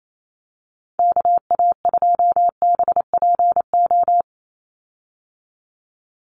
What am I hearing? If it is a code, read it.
Morse code, heard as a keyed tone, about 28 wpm, 700 Hz. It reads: KA2BPO
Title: USA Call Signs 28wpm